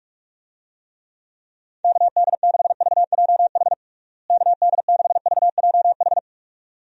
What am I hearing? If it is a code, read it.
Morse code, heard as a keyed tone, about 46 wpm, 700 Hz. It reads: KD6VJH KD6VJH
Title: USA Call Signs - 6 Characters - Repeated 2x - 45wpm